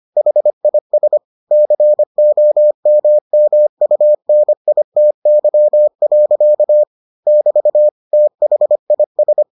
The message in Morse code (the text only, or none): HIS COMMUNITY. <BT> THIS